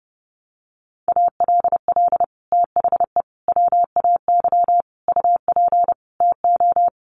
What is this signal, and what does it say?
Morse code, heard as a keyed tone, about 30 wpm, 700 Hz. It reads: ALL THE WAY UP TO
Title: Sets of 5 Words Encore 30wpm